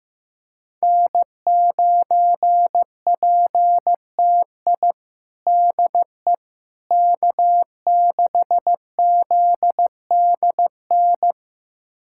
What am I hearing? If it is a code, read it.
Morse code, heard as a keyed tone, about 15 wpm, 700 Hz. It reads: N9PTI DE K6ZDN
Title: USA Callsign DE Callsign - 5 Characters - 15wpm